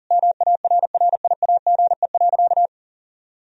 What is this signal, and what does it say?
Morse code, heard as a keyed tone, about 40 wpm, 700 Hz. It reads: MARRIAGE.
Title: Sentences from Top 1000 Words - Review Part 1 - 40wpm